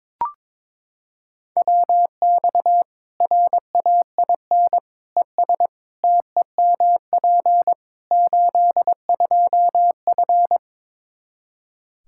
Morse code, heard as a keyed tone, about 22 wpm, 700 Hz. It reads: WX RAIN ES TEMP 82F
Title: Practice QSO - Exchange 2 - Info - 22wpm